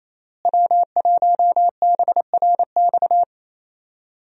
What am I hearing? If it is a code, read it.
Morse code, heard as a keyed tone, about 28 wpm, 700 Hz. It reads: W1BRX
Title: USA Call Signs - 5 Characters - Repeated 3x - 28wpm